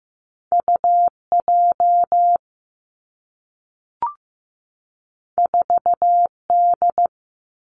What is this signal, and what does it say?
Morse code, heard as a keyed tone, about 15 wpm, 700 Hz. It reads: U J 4 D
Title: Two Challenging Letter-Number Combinations 15wpm